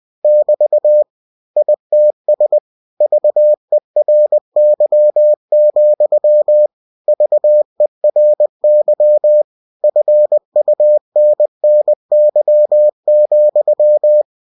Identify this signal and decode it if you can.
Morse code, heard as a keyed tone, about 20 wpm, 600 Hz. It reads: <BT> ITS VERY, VERY FUNNY,
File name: cw-books / winnie-the-pooh / winnie-20wpm-page-0039